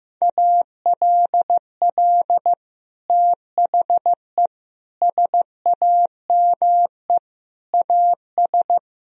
Morse code, heard as a keyed tone, about 15 wpm, 700 Hz. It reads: ALL THE SAME AS
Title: Sentences from Top 1000 Words - Review Part 2 - 15wpm